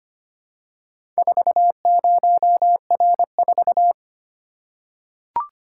Morse code, heard as a keyed tone, about 25 wpm, 700 Hz. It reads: 40R4
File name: practice / 25wpm / Four Letter-Number Combinations - 25wpm